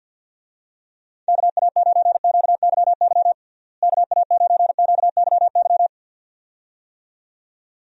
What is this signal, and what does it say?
Morse code, heard as a keyed tone, about 46 wpm, 700 Hz. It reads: KA9QYY KA9QYY
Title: USA Call Signs - 6 Characters - Repeated 2x - 50wpm